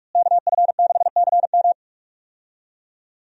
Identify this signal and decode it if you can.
Morse code, heard as a keyed tone, about 46 wpm, 700 Hz. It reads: KF6CM
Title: USA Call Signs - 5 Characters - Repeated 2x - 45wpm